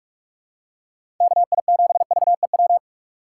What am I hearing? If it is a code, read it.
Morse code, heard as a keyed tone, about 46 wpm, 700 Hz. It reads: KI7VEW
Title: USA Call Signs - 6 Characters - 2x Character Spacing - 45wpm